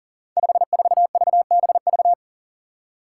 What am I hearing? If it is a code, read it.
Morse code, heard as a keyed tone, about 39 wpm, 700 Hz. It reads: 54VBV
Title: Five Challenging Letter-Number Combinations 40wpm